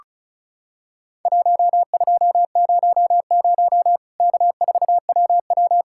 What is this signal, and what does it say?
Morse code, heard as a keyed tone, about 36 wpm, 700 Hz. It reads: 1200 K4WW
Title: Worked All Europe DX Contest - QTC Exchange - 35wpm